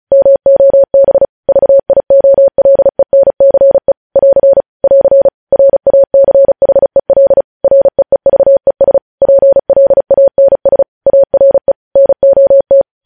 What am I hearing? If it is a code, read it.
Morse code, heard as a keyed tone, about 36 wpm, 550 Hz. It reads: MOB VIOLENCE <AR> <AR> RACHEL REEVES PLANS ARE NOT
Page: news headlines in morse code at 35 wpm